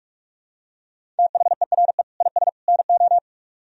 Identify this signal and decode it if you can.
Morse code, heard as a keyed tone, about 44 wpm, 700 Hz. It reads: THERE IS NO